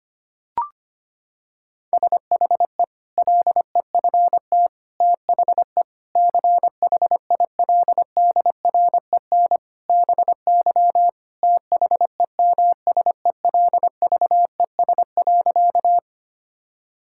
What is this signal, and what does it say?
Morse code, heard as a keyed tone, about 25 wpm, 700 Hz. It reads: SHE LEFT THE CHILDREN BY THEMSELVES.
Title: Sentences from Top 1000 Words - Review Part 5 - 25wpm